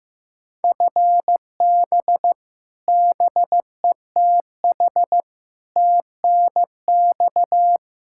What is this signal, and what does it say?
Morse code, heard as a keyed tone, about 15 wpm, 700 Hz. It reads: FB BETH TNX
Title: Practice QSO - Exchange 3 - More Info - 15wpm